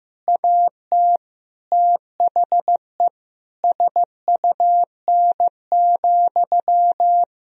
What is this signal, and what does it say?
Morse code, heard as a keyed tone, about 15 wpm, 700 Hz. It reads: AT THE SUN,
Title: Sentences from Top 1000 Words - Review Part 1 - 15wpm